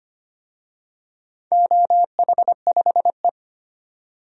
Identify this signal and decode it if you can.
Morse code, heard as a keyed tone, about 25 wpm, 700 Hz. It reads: OH5E
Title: DXCC-100 Call Signs - 4 Characters - 25wpm